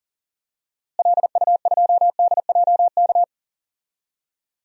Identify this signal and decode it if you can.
Morse code, heard as a keyed tone, about 37 wpm, 700 Hz. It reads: LU2DJK
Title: DXCC-100 Call Signs - 6 Characters - Repeated 3x - 40wpm